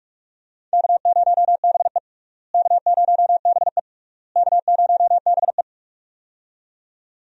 Morse code, heard as a keyed tone, about 46 wpm, 700 Hz. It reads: K0BE K0BE K0BE
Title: USA Call Signs - 4 Characters - Repeated 3x - 45wpm